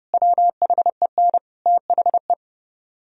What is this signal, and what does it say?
Morse code, heard as a keyed tone, about 30 wpm, 700 Hz. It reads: WHEN THE